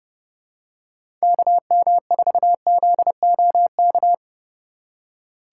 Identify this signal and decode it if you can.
Morse code, heard as a keyed tone, about 31 wpm, 700 Hz. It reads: KM4ZOK
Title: USA Call Signs - 6 Characters - 3x Character Spacing - 30wpm